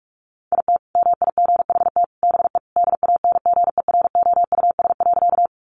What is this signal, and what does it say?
Morse code, heard as a keyed tone, about 44 wpm, 700 Hz. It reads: IT MIGHT BE DANGEROUS.
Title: Sentences from Top 1200 Words 45wpm